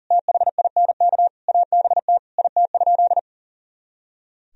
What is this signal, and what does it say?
Morse code, heard as a keyed tone, about 40 wpm, 700 Hz. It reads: THINK ABT IT?
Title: Sentences from Top 200 Words and Abbreviations 40wpm